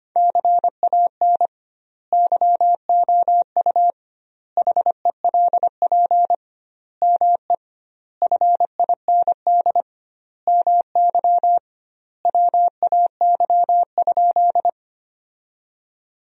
Morse code, heard as a keyed tone, about 25 wpm, 700 Hz. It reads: CAN YOU HELP ME FIND MY WAY?
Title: 2x Word-Spacing - Sentences from Top 200 Words - 25wpm